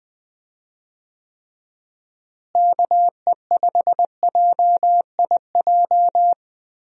Morse code, heard as a keyed tone, about 20 wpm, 700 Hz. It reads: KE5JIJ
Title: USA Call Signs - 6 Characters - Repeated 3x - 20wpm